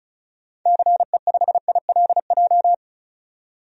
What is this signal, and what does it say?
Morse code, heard as a keyed tone, about 36 wpm, 690 Hz. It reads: CE5ILJ
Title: DXCC-100 Call Signs - 6 Characters - 35wpm